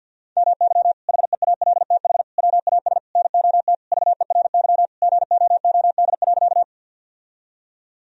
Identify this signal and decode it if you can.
Morse code, heard as a keyed tone, about 49 wpm, 700 Hz. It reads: MY HEALTH WAS NOT VERY GOOD.
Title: Sentences from Top 1000 Words - Review Part 3 - 50wpm